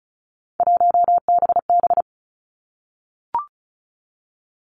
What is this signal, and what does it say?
Morse code, heard as a keyed tone, about 36 wpm, 700 Hz. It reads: 1BB
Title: Three Challenging Letter-Number Combinations 35wpm